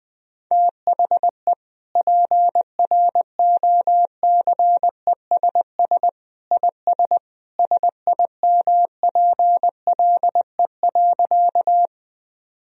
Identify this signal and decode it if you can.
Morse code, heard as a keyed tone, about 20 wpm, 700 Hz. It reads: THE PROCESS IS SIMPLE.
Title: Sentences from Top 1000 Words - Review Part 3 - 20wpm